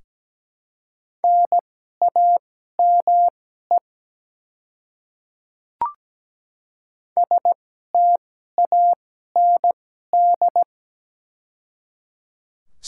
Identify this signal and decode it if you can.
Morse code, heard as a keyed tone, about 17 wpm, spaced as for 12 wpm, 700 Hz. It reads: NAME STAND
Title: 2x Character Spacing - Top 300 Words - 17wpm